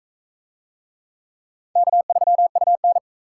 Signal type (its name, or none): Morse code